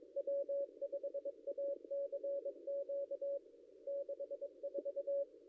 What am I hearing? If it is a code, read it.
Morse code, heard as a keyed tone, about 23 wpm, 560 Hz. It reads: W5ACQ 64
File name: contests / ss-cw-k5tr-op-n5rz -0800